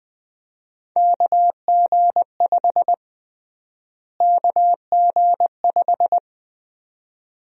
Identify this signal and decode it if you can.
Morse code, heard as a keyed tone, about 20 wpm, 700 Hz. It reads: KG5 KG5
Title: USA Call Sign Prefixes - Repeated 2x - 20wpm